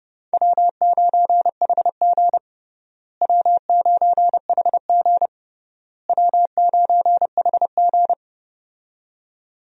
Morse code, heard as a keyed tone, about 30 wpm, 700 Hz. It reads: W9HG W9HG W9HG
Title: USA Call Signs - 4 Characters - Repeated 3x - 30wpm